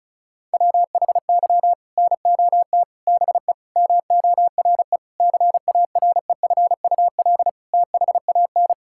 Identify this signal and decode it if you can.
Morse code, heard as a keyed tone, about 34 wpm, 700 Hz. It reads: WHY NOT BE MORE CAREFUL THAN
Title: Sentences from Top 1600 Words 35wpm